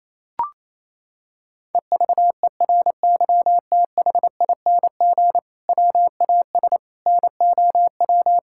Morse code, heard as a keyed tone, about 28 wpm, 700 Hz. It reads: EVERYTHING WAS NOW
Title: Sentences from Top 1000 Words - Review Part 5 - 28wpm